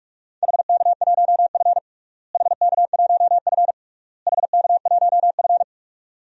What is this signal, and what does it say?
Morse code, heard as a keyed tone, about 46 wpm, 700 Hz. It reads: HK1F HK1F HK1F
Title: DXCC-100 Call Signs - 4 Characters - Repeated 3x - 45wpm